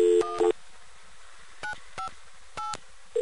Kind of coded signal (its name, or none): DTMF